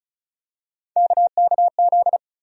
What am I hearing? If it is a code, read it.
Morse code, heard as a keyed tone, about 36 wpm, 700 Hz. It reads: KKZ
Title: USA Call Sign Postfixes - 3 Letters 35wpm